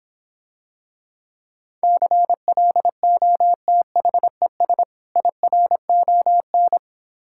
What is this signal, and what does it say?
Morse code, heard as a keyed tone, about 26 wpm, 700 Hz. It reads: CLOTHES IRON